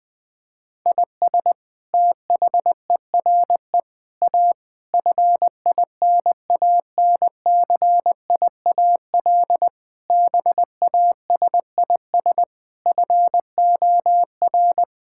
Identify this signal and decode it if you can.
Morse code, heard as a keyed tone, about 20 wpm, 700 Hz. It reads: IS THERE A FINANCIAL BASIS FOR
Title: Sentences from Top 1000 Words - Review Part 5 - 20wpm